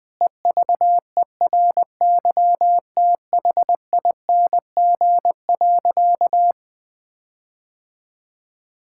Morse code, heard as a keyed tone, about 21 wpm, 700 Hz. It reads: EVERYTHING.
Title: All-Age Appropriate Jokes - 20wpm